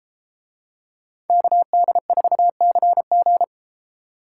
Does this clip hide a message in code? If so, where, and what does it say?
Yes: Morse code from 1.3 to 3.4 s, heard as a keyed tone, about 32 wpm, 700 Hz. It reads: KD4CG